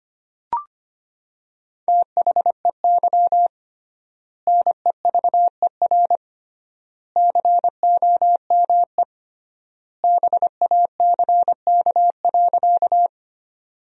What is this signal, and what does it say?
Morse code, heard as a keyed tone, about 25 wpm, 700 Hz. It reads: THEY NEVER COME BACK.
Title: 3x Word-Spacing - Sentences from Top 300 Words - 25wpm